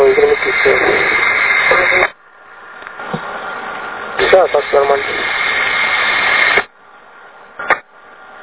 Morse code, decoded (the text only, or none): none